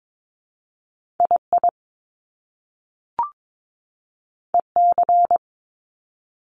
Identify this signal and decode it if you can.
Morse code, heard as a keyed tone, about 22 wpm, 700 Hz. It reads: II EC
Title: Top Two Letter Combinations in Words 22wpm